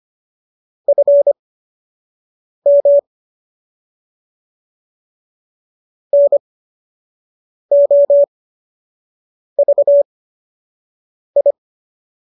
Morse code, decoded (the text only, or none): FM NOVI